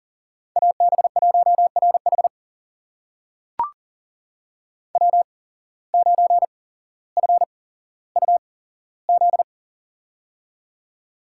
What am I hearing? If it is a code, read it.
Morse code, heard as a keyed tone, about 40 wpm, spaced as for 17 wpm, 700 Hz. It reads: AB1RH W9FUZ
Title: USA Call Signs - 5 Characters - 8x Character Spacing - 40wpm